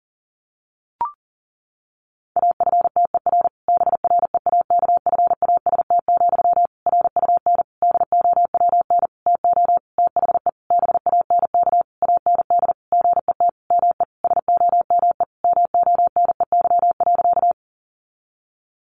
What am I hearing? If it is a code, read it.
Morse code, heard as a keyed tone, about 40 wpm, 700 Hz. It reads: AFTER BREAKFAST, RUN DOWN TO THE BANK AND GET ME SOME MONEY.
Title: Sentences from Top 1500 Words 40wpm